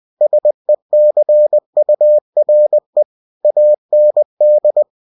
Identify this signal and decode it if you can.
Morse code, heard as a keyed tone, about 20 wpm, 600 Hz. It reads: SECURE AND
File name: cw-books / winnie-the-pooh / winnie-20wpm-page-0120